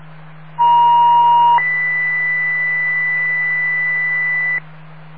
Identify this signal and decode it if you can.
Two-tone page, heard as a one-second tone, then a three-second tone, about 950 then 2000 Hz. It is heard from 0.6 to 4.6 s.